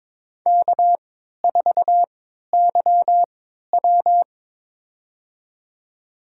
Morse code, heard as a keyed tone, about 22 wpm, spaced as for 13 wpm, 700 Hz. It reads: K4YW
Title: USA Call Signs - 4 Characters - 3x Character Spacing - 22wpm